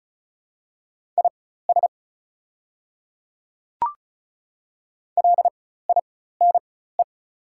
Morse code, heard as a keyed tone, about 36 wpm, spaced as for 16 wpm, 700 Hz. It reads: IS LINE
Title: 4x Character Spacing - Top 300 Words - 35wpm